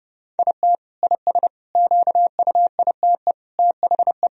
Morse code, heard as a keyed tone, about 30 wpm, 700 Hz. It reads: IT IS QUITE THE
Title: Sentences from Top 1000 Words - Review Part 2 - 30wpm